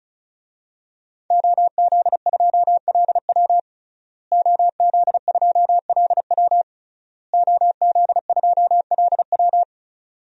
Morse code, heard as a keyed tone, about 34 wpm, 700 Hz. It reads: OZ2LW OZ2LW OZ2LW
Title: DXCC-100 Call Signs - 5 Characters - Repeated 3x - 35wpm